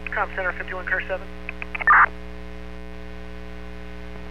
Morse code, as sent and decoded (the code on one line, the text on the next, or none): none